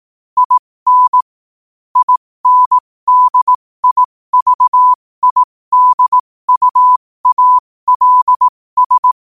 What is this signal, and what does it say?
Morse code, heard as a keyed tone, about 18 wpm, spaced as for 15 wpm, 1000 Hz. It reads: IN INDIVIDUALS
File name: qotdfiles / 15wpm